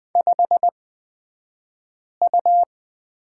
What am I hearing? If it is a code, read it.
Morse code, heard as a keyed tone, about 20 wpm, spaced as for 5 wpm, 700 Hz. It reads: 5U
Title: 254 Lesson 41d - Period - Two Characters 20wpm